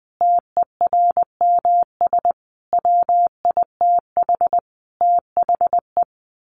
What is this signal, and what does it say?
Morse code, heard as a keyed tone, about 20 wpm, 700 Hz. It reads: TERMS WITH THE
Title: Sets of 5 Words Encore2 20wpm